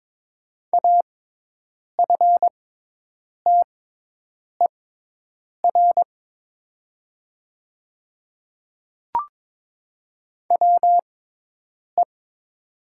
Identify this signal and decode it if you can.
Morse code, heard as a keyed tone, about 22 wpm, spaced as for 8 wpm, 700 Hz. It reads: AFTER WE
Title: 6x Character Spacing - Top 500 Words - 22wpm